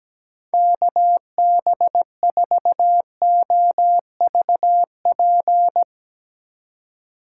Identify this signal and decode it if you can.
Morse code, heard as a keyed tone, about 17 wpm, 700 Hz. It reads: KB4OVP
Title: USA Call Signs - 6 Characters - 2x Character Spacing - 17wpm